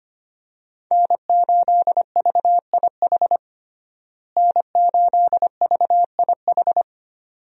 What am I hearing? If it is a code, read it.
Morse code, heard as a keyed tone, about 25 wpm, 700 Hz. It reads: N8VIH N8VIH
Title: USA Call Signs - 5 Characters - Repeated 2x - 25wpm